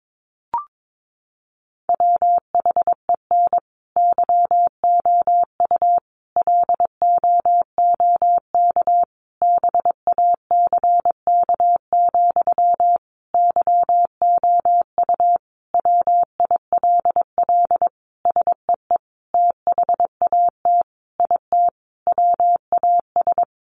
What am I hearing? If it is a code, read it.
Morse code, heard as a keyed tone, about 22 wpm, 700 Hz. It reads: WHEN YOU LOOK BACK, YOU WILL SEE THAT IT WAS
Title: Sentences from Top 800 Words 22wpm